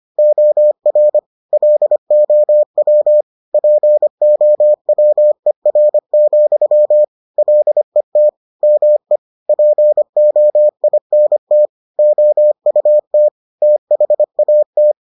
Morse code, heard as a keyed tone, about 25 wpm, 600 Hz. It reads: OR LOW POWER, LET ME POINT OUT THAT